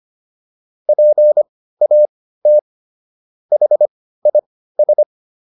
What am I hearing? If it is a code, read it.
Morse code, heard as a keyed tone, about 25 wpm, spaced as for 15 wpm, 600 Hz. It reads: PAT HIS